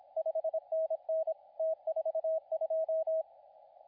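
Morse code, heard as a keyed tone, about 26 wpm, 640 Hz, with radio noise. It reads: 5NN T42
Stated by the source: CW contest audio recorded at LA4C during the SAC CW contest.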